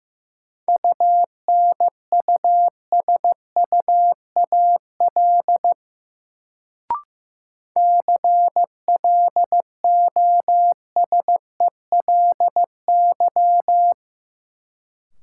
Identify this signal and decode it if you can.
Morse code, heard as a keyed tone, about 15 wpm, 700 Hz. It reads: UNUSUAL CLOSELY
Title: Sentences from Top 1400 Words - New Word Drill Alt - 15wpm